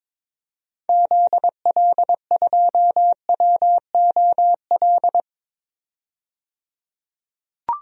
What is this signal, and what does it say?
Morse code, heard as a keyed tone, about 22 wpm, 700 Hz. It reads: ZL2WOL
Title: DXCC-100 Call Signs - 6 Characters - 22wpm